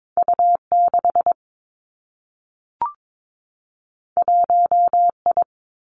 Morse code, heard as a keyed tone, about 22 wpm, 700 Hz. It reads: U 6 1 I
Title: Two Challenging Letter-Number Combinations 22wpm